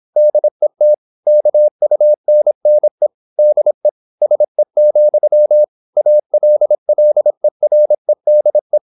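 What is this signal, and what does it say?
Morse code, heard as a keyed tone, about 26 wpm, 600 Hz. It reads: DET KUNNE DE SE, ALLEREDE
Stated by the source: High-speed (QRQ) Morse at 26 wpm.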